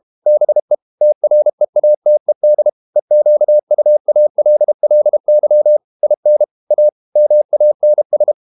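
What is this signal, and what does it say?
Morse code, heard as a keyed tone, about 32 wpm, 600 Hz. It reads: BE TREATED EQUALLY IN A MANS
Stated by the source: Morse code practice at 32 wpm from CW med Gnister, reading Ken Follett's novel Never in English.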